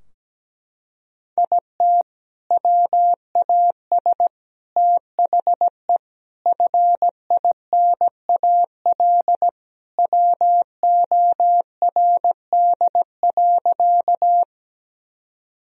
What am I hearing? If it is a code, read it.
Morse code, heard as a keyed tone, about 17 wpm, 700 Hz. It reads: IT WAS THE FINAL WORD.
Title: Sentences from Top 1000 Words - Review Part 2 - 17wpm